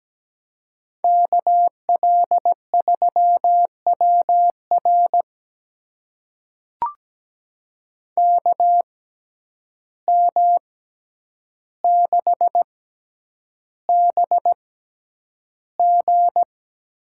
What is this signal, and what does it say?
Morse code, heard as a keyed tone, about 17 wpm, spaced as for 9 wpm, 700 Hz. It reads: KL3WR KM6BG
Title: USA Call Signs - 6x Character Spacing - 17wpm